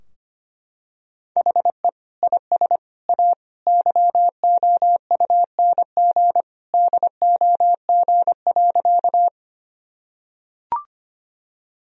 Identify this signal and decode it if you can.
Morse code, heard as a keyed tone, about 25 wpm, 700 Hz. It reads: HE IS A YOUNG DOG.
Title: Sentences from Top 1000 Words - Review Part 2 - 25wpm